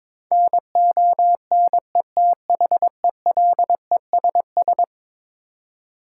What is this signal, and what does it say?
Morse code, heard as a keyed tone, about 22 wpm, 700 Hz. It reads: NONETHELESS
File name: practice / 22wpm / Eleven Letter Words - 22wpm